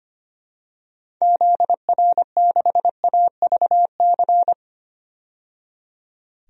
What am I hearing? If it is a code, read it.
Morse code, heard as a keyed tone, about 25 wpm, 700 Hz. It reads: ZR6AVC
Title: DXCC-100 Call Signs - 6 Characters - 25wpm